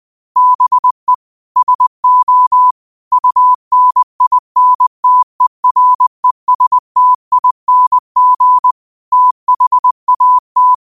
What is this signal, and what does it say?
Morse code, heard as a keyed tone, about 20 wpm, 1000 Hz. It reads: BE SO UNINTERESTING THAT